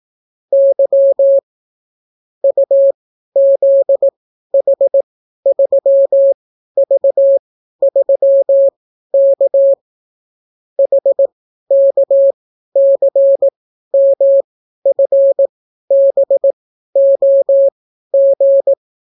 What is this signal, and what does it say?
Morse code, heard as a keyed tone, about 18 wpm, spaced as for 12 wpm, 550 Hz. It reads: Y UZH3V3K HKCMFBOG